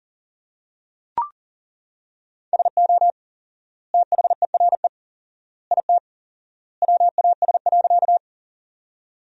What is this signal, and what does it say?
Morse code, heard as a keyed tone, about 37 wpm, 700 Hz. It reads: SO THERE IT WAS.
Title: 4x Word-Spacing - Sentences from Top 100 Words - 40wpm